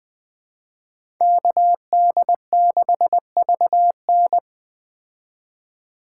Morse code, heard as a keyed tone, about 20 wpm, 700 Hz. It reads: KD6VN